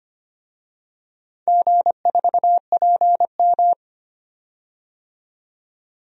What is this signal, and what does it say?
Morse code, heard as a keyed tone, about 25 wpm, 700 Hz. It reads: G4PM
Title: DXCC-100 Call Signs - 4 Characters - 25wpm